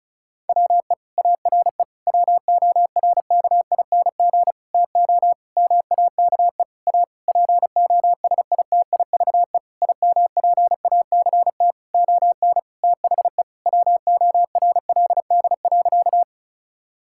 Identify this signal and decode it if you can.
Morse code, heard as a keyed tone, about 36 wpm, 700 Hz. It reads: WE ARE WORKING TO MAKE A POSITIVE IMPACT ON THE WORLD.
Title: Sentences from Top 1000 Words - Review Part 6 - 35wpm